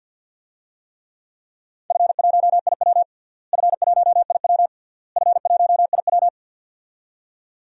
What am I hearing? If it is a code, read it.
Morse code, heard as a keyed tone, about 51 wpm, 700 Hz. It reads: F1IW F1IW F1IW
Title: DXCC-100 Call Signs - 4 Characters - Repeated 3x - 50wpm